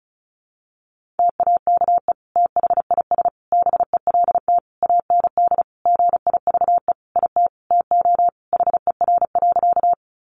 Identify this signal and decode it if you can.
Morse code, heard as a keyed tone, about 34 wpm, 700 Hz. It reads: TAKE THIS BELT AND GIVE IT TO HER.
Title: Sentences from Top 1400 Words 35wpm